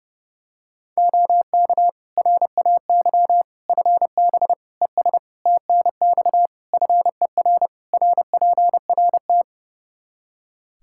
Morse code, heard as a keyed tone, about 32 wpm, 700 Hz. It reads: OK RAY FB ES TNX FER RPRT
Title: Practice QSO - Exchange 2 - Info - 30wpm